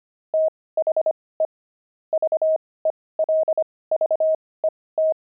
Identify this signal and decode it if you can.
Morse code, heard as a keyed tone, about 25 wpm, spaced as for 18 wpm, 630 Hz. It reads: THE VELVET
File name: pangram-fs-18-cs-25-fragment